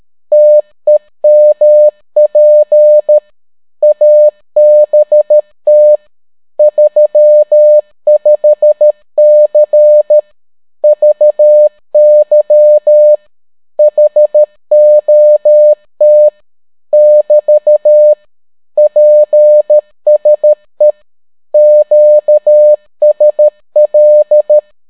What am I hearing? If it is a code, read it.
Morse code, heard as a keyed tone, about 13 wpm, 600 Hz. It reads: TEMP ABT 35C VY HOT <BT> PSE QSL